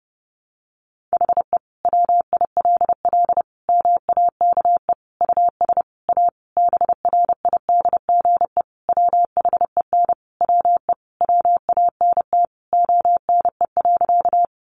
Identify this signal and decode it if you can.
Morse code, heard as a keyed tone, about 30 wpm, 700 Hz. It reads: HE WILL MAKE US A BRIDGE WHEN WE WANT ONE.
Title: Sentences from Top 1200 Words 30wpm